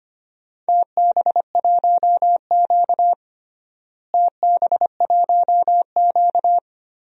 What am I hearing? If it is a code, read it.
Morse code, heard as a keyed tone, about 25 wpm, 700 Hz. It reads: TB1Q TB1Q
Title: DXCC-100 Call Signs - 4 Characters - Repeated 3x - 25wpm